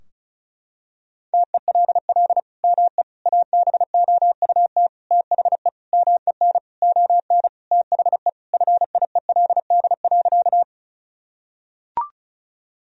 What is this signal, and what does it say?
Morse code, heard as a keyed tone, about 36 wpm, 700 Hz. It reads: TELL ME ABOUT THE MEN ON THE FIELD.
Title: Sentences from Top 1000 Words - Review Part 2 - 35wpm